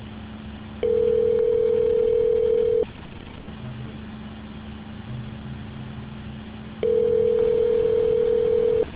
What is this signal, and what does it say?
Ringback tone (North America), 440 + 480 Hz.